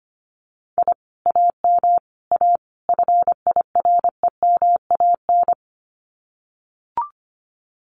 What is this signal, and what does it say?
Morse code, heard as a keyed tone, about 25 wpm, 700 Hz. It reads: I AM A FIREMAN
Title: Common QSO Phrases Encore 25wpm